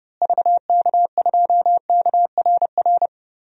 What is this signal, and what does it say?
Morse code, heard as a keyed tone, about 30 wpm, 700 Hz. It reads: VK2KRR
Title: DXCC-100 Call Signs - 6 Characters - Repeated 2x - 30wpm